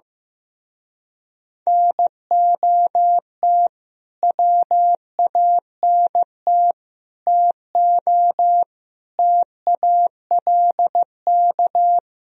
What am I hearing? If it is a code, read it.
Morse code, heard as a keyed tone, about 15 wpm, 700 Hz. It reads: NOT WANT TO TALK
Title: Speed-Racing - Sets of 5 Words 15wpm